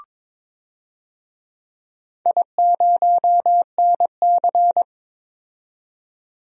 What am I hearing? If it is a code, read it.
Morse code, heard as a keyed tone, about 22 wpm, 700 Hz. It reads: I0NC
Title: DXCC-100 Call Signs - 4 Characters - 22wpm